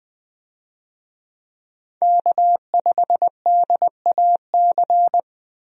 Morse code, heard as a keyed tone, about 20 wpm, 700 Hz. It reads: K5DAC
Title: USA Call Signs - 5 Characters - 20wpm